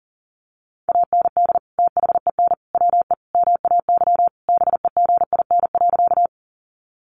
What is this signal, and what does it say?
Morse code, heard as a keyed tone, about 40 wpm, 700 Hz. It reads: AND THEN WE MAY BEGIN.
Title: Sentences from Top 300 Words 40wpm